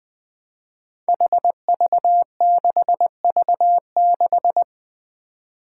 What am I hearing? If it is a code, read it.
Morse code, heard as a keyed tone, about 20 wpm, 700 Hz. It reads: HV6V6
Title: Five Challenging Letter-Number Combinations 20wpm